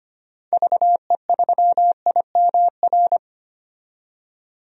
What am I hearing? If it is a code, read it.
Morse code, heard as a keyed tone, about 25 wpm, 700 Hz. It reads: VE3IMR
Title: DXCC-100 Call Signs - 6 Characters - 25wpm